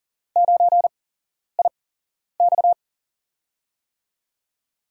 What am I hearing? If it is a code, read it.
Morse code, heard as a keyed tone, about 41 wpm, spaced as for 11 wpm, 690 Hz. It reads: 9IX
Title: USA Call Signs - 4 Characters - 8x Character Spacing - 40wpm